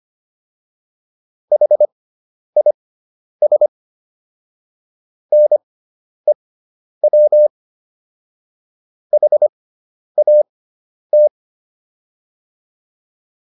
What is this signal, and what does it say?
Morse code, heard as a keyed tone, about 25 wpm, spaced as for 10 wpm, 600 Hz. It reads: HIS NEW HAT